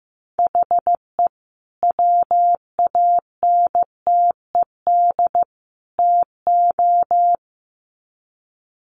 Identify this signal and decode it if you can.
Morse code, heard as a keyed tone, about 15 wpm, 700 Hz. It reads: HE WANTED TO
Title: Sets of 3 Words 15wpm